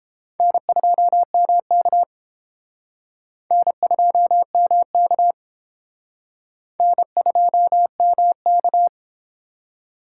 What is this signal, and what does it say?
Morse code, heard as a keyed tone, about 29 wpm, 700 Hz. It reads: N2MK N2MK N2MK